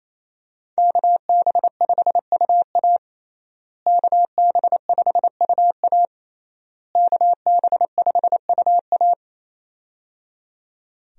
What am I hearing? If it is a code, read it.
Morse code, heard as a keyed tone, about 28 wpm, 700 Hz. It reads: KB5UA KB5UA KB5UA
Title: USA Call Signs - 5 Characters - Repeated 3x - 28wpm